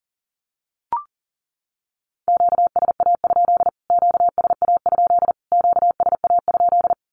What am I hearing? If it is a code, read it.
Morse code, heard as a keyed tone, about 40 wpm, 700 Hz. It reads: QSA? QSA? QSA?
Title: Q-Codes Repeated 3x 40wpm